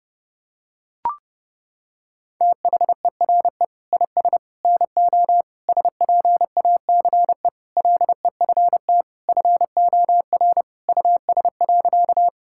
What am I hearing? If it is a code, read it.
Morse code, heard as a keyed tone, about 30 wpm, 700 Hz. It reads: THERE IS NO SPACE LEFT FOR US.
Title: Sentences from Top 1000 Words - Review Part 5 - 30wpm